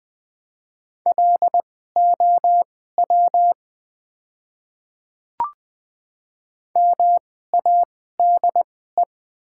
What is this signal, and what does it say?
Morse code, heard as a keyed tone, about 20 wpm, spaced as for 14 wpm, 700 Hz. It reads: LOW MADE
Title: 2x Character Spacing - Top 200 Words - 20wpm